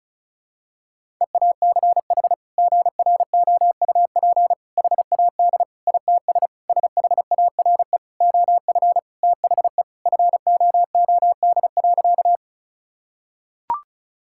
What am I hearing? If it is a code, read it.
Morse code, heard as a keyed tone, about 36 wpm, 700 Hz. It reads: EACH GROUP HAD ITS SHARE OF THE FOOD.
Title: Sentences from Top 1000 Words - Review Part 6 - 35wpm